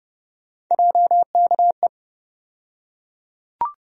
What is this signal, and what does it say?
Morse code, heard as a keyed tone, about 30 wpm, 700 Hz. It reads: JKE
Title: USA Call Sign Postfixes - 3 Letters 30wpm